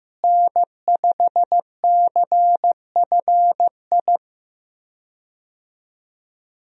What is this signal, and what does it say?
Morse code, heard as a keyed tone, about 15 wpm, 700 Hz. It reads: N5CFI
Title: USA Call Signs - 5 Characters - Repeated 3x - 15wpm